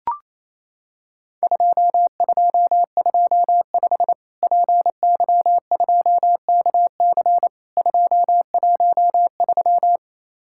Morse code, heard as a keyed tone, about 28 wpm, 700 Hz. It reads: 2225 PY2KC 213
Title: Worked All Europe DX Contest - QTC Exchange - 28wpm